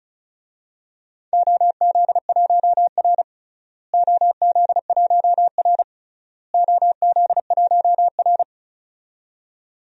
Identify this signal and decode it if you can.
Morse code, heard as a keyed tone, about 36 wpm, 700 Hz. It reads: OZ1R OZ1R OZ1R
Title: DXCC-100 Call Signs - 4 Characters - Repeated 3x - 35wpm